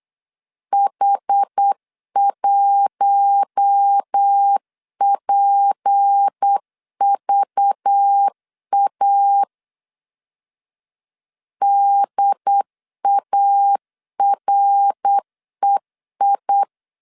Morse code, heard as a keyed tone, about 8 wpm, 800 Hz. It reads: H1PVA DAREI
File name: LEZIONI / 6 PKX0 / 6 P 8wpm 3025